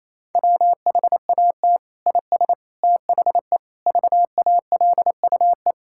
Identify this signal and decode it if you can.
Morse code, heard as a keyed tone, about 27 wpm, 700 Hz. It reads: WHAT IS THE VALUE
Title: Sentences from Top 1000 Words - Review Part 6 - 28wpm